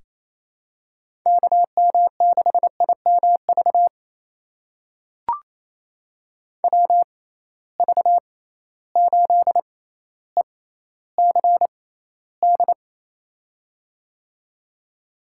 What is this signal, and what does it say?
Morse code, heard as a keyed tone, about 28 wpm, spaced as for 14 wpm, 700 Hz. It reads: KM6IMV WV8ECD
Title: USA Call Signs - 6 Characters - 6x Character Spacing - 28wpm